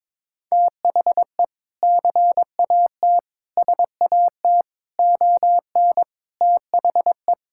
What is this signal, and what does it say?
Morse code, heard as a keyed tone, about 22 wpm, 700 Hz. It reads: THE CAT SAT ON THE
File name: practice / 22wpm / Sentences with 2-Word Phrases - 22wpm